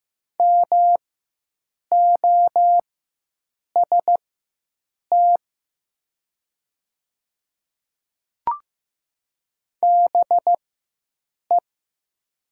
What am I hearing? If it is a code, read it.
Morse code, heard as a keyed tone, about 15 wpm, spaced as for 7 wpm, 700 Hz. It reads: MOST BE
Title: 4x Character Spacing - Top 200 Words - 15wpm